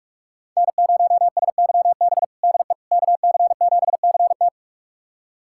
Morse code, heard as a keyed tone, about 46 wpm, 700 Hz. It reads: N0SYB DE KC7CT
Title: USA Callsign DE Callsign - 5 Characters - 45wpm